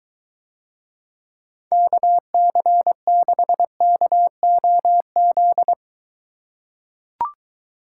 Morse code, heard as a keyed tone, about 22 wpm, 700 Hz. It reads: KC6KOZ